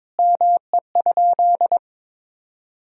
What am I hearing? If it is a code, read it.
Morse code, heard as a keyed tone, about 22 wpm, 700 Hz. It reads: ME?
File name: practice / 22wpm / Sentences with 2-Word Phrases - 22wpm